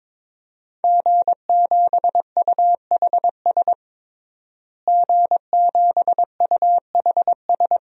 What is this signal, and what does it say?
Morse code, heard as a keyed tone, about 22 wpm, 700 Hz. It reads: G7UHS G7UHS
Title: DXCC-100 Call Signs - 5 Characters - Repeated 3x - 22wpm